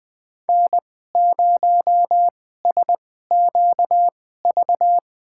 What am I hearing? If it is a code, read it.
Morse code, heard as a keyed tone, about 20 wpm, spaced as for 14 wpm, 700 Hz. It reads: N0SQV
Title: USA Call Signs - 2x Character Spacing - 20wpm